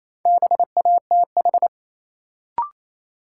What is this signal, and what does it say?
Morse code, heard as a keyed tone, about 28 wpm, 700 Hz. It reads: BATH
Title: Sentences from Top 1600 Words - New Word Drill - 28wpm